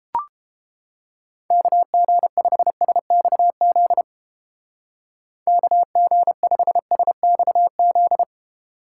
Morse code, heard as a keyed tone, about 30 wpm, 700 Hz. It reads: KG5SXZ KG5SXZ